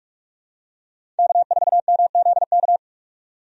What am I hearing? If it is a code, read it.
Morse code, heard as a keyed tone, about 42 wpm, 700 Hz. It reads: K4MZK
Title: USA Call Signs - 5 Characters - 45wpm